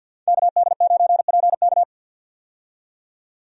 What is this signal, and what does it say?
Morse code, heard as a keyed tone, about 52 wpm, 700 Hz. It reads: KD9PK
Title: USA Call Signs - 5 Characters - 50wpm